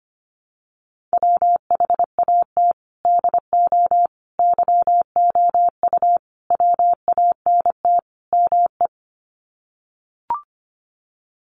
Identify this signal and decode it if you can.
Morse code, heard as a keyed tone, about 25 wpm, 700 Hz. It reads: WHAT DO YOU WANT ME
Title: Sets of 5 Words Encore2 25wpm